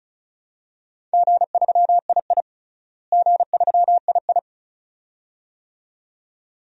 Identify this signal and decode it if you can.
Morse code, heard as a keyed tone, about 36 wpm, 700 Hz. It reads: G3II G3II
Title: DXCC-100 Call Signs - 4 Characters - Repeated 2x - 35wpm